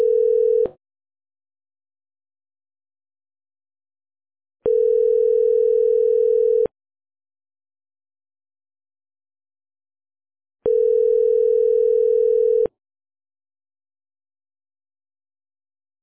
Ringback tone (North America), 440 + 480 Hz.